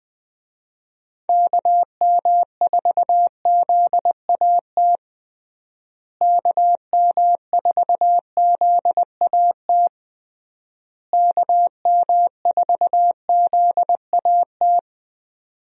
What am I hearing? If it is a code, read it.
Morse code, heard as a keyed tone, about 21 wpm, 700 Hz. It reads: KM4ZAT KM4ZAT KM4ZAT
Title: USA Call Signs - 6 Characters - Repeated 3x - 20wpm